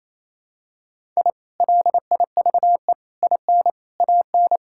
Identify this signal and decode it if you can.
Morse code, heard as a keyed tone, about 28 wpm, 700 Hz. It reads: I LIVE IN AN
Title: Sentences from Top 1000 Words - Review Part 6 - 28wpm